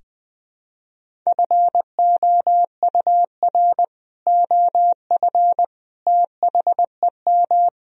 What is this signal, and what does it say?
Morse code, heard as a keyed tone, about 20 wpm, 700 Hz. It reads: FOUR OF THEM
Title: Sentences from Top 1000 Words - Review Part 1 - 20wpm